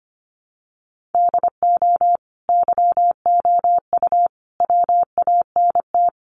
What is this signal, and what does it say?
Morse code, heard as a keyed tone, about 25 wpm, 700 Hz. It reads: DO YOU WANT
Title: Sets of 5 Words 25wpm